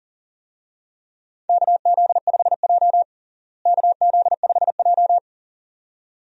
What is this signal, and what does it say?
Morse code, heard as a keyed tone, about 40 wpm, 700 Hz. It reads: KZ5J KZ5J
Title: USA Call Signs - 4 Characters - Repeated 2x - 40wpm